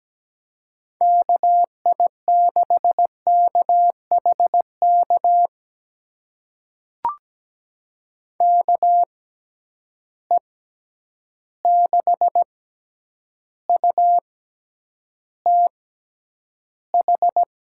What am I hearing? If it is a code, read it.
Morse code, heard as a keyed tone, about 17 wpm, spaced as for 9 wpm, 700 Hz. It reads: KI6KHK KE6UTH
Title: USA Call Signs - 6 Characters - 6x Character Spacing - 17wpm